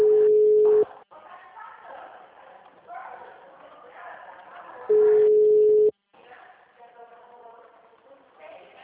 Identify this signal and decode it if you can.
Ringback tone (Europe, 425 Hz), with speech.
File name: out- -2010- - - .46257